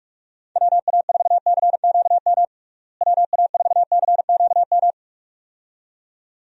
Morse code, heard as a keyed tone, about 45 wpm, 700 Hz. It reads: WA4CQM WA4CQM
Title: USA Call Signs - 6 Characters - Repeated 2x - 45wpm